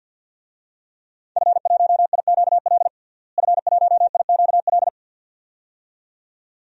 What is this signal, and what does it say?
Morse code, heard as a keyed tone, about 46 wpm, 700 Hz. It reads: F1IQL F1IQL
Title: DXCC-100 Call Signs - 5 Characters - Repeated 2x - 50wpm